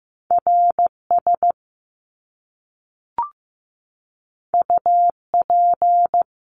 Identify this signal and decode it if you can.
Morse code, heard as a keyed tone, about 15 wpm, 700 Hz. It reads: RS UP
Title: Top Two Letter Combinations in Words 15wpm